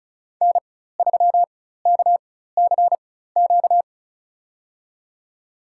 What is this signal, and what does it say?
Morse code, heard as a keyed tone, about 36 wpm, spaced as for 16 wpm, 700 Hz. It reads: N3KCQ
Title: USA Call Signs - 5 Characters - 4x Character Spacing - 35wpm